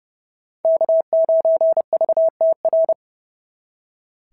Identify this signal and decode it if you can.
Morse code, heard as a keyed tone, about 29 wpm, 690 Hz. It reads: K9VTR
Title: SOTA QSO Practice 30wpm